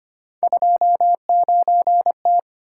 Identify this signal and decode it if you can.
Morse code, heard as a keyed tone, about 25 wpm, 700 Hz. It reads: 29T
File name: Three Letter-Number Combinations - 25wpm